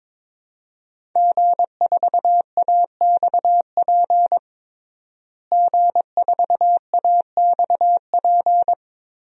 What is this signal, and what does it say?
Morse code, heard as a keyed tone, about 22 wpm, 700 Hz. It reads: G4AXP G4AXP
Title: DXCC-100 Call Signs - 5 Characters - Repeated 2x - 22wpm